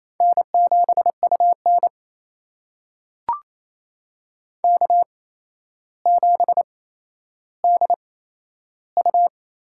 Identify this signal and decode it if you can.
Morse code, heard as a keyed tone, about 28 wpm, spaced as for 12 wpm, 700 Hz. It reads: N7UN K7DU
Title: USA Call Signs - 4 Characters - 8x Character Spacing - 28wpm